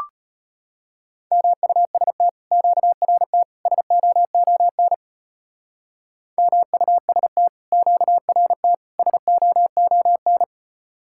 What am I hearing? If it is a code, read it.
Morse code, heard as a keyed tone, about 33 wpm, 700 Hz. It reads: MUST QRT SOON MUST QRT SOON